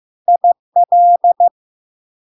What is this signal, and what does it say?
Morse code, heard as a keyed tone, about 15 wpm, 700 Hz. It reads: IL
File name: LARK-FD-2022-Log-in-CW-15wpm-700-Hz